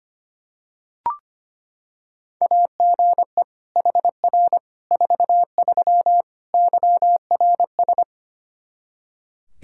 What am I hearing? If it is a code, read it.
Morse code, heard as a keyed tone, about 25 wpm, 700 Hz. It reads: AGE HR 43 YRS
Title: Practice QSO - Exchange 3 - More Info - 25wpm